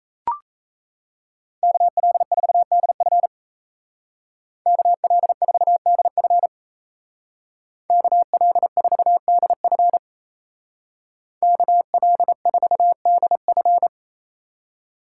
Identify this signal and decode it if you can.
Morse code, heard as a keyed tone, about 36 wpm, 700 Hz. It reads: KL4DF KL4DF KL4DF KL4DF